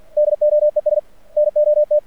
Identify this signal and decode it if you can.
Morse code, heard as a keyed tone, about 45 wpm, 600 Hz. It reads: DOES NOT
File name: archived / CRSnet